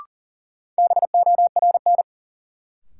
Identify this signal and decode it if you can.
Morse code, heard as a keyed tone, about 40 wpm, 700 Hz. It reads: BORN
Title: Sentences from Top 1000 Words - New Word Drill - 40wpm